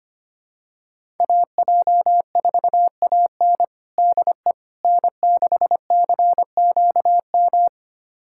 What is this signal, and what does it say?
Morse code, heard as a keyed tone, about 25 wpm, 700 Hz. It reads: AJ4AN DE N6CQM
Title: USA Callsign DE Callsign - 5 Characters - 25wpm